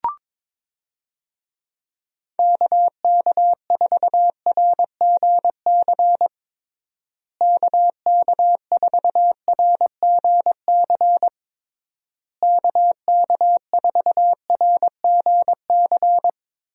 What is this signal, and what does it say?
Morse code, heard as a keyed tone, about 22 wpm, 700 Hz. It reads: KK4RGC KK4RGC KK4RGC
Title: USA Call Signs - 6 Characters - Repeated 3x - 22wpm